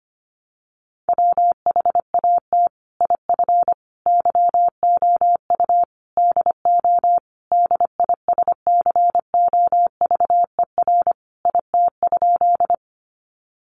Morse code, heard as a keyed tone, about 25 wpm, 700 Hz. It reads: WHAT IF YOU DO DISCOVER IT?
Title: Sentences from Top 1100 Words 25wpm